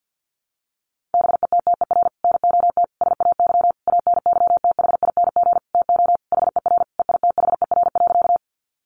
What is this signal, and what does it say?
Morse code, heard as a keyed tone, about 50 wpm, 700 Hz. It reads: BETTER NOT SAY ANYTHING TO HER EITHER.
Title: Sentences from Top 1000 Words 50wpm